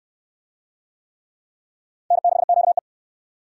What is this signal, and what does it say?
Morse code, heard as a keyed tone, about 61 wpm, 700 Hz. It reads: N6QE